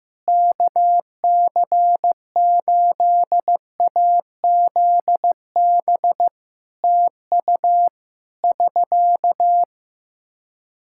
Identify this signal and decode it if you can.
Morse code, heard as a keyed tone, about 15 wpm, 700 Hz. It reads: KC8AZB TU <SK>
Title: Practice QSO - Exchange 4 - Ending - 15wpm